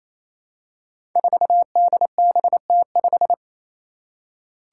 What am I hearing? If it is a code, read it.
Morse code, heard as a keyed tone, about 28 wpm, 700 Hz. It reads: 4DBT5
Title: Five Challenging Letter-Number Combinations 28wpm